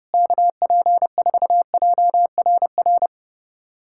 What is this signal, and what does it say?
Morse code, heard as a keyed tone, about 29 wpm, 700 Hz. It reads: KP4JRR
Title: USA Callsign DE Callsign - 6 Characters - 30wpm